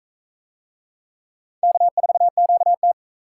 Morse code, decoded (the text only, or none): K4QT